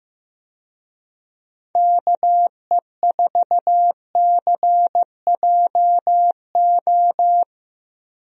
Morse code, heard as a keyed tone, about 15 wpm, 700 Hz. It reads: KE4CJO
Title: USA Call Signs - 6 Characters - Repeated 2x - 15wpm